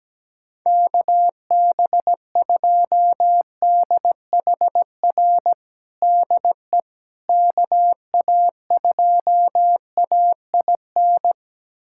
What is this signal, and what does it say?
Morse code, heard as a keyed tone, about 17 wpm, 700 Hz. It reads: KB2DHR DE KA2AIN
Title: USA Callsign DE Callsign - 6 Characters - 17wpm